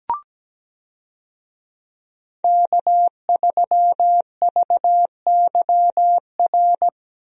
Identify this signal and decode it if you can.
Morse code, heard as a keyed tone, about 17 wpm, 700 Hz. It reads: K3VYR
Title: USA Call Signs - 5 Characters - Repeated 3x - 17wpm